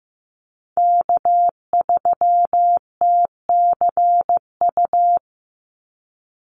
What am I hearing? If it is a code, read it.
Morse code, heard as a keyed tone, about 15 wpm, 700 Hz. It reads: K3TCU
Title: NA SOTA Chaser Callsigns 15wpm